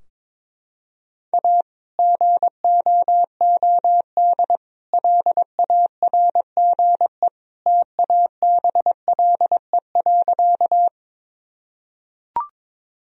Morse code, heard as a keyed tone, about 21 wpm, 700 Hz. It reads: A GOOD LARGE TABLE.
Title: Sentences from Top 1000 Words - Review Part 2 - 22wpm